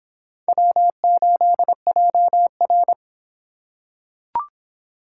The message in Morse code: W8JR

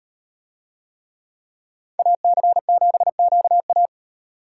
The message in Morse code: AC7QA

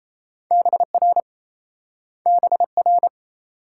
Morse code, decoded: BR BR

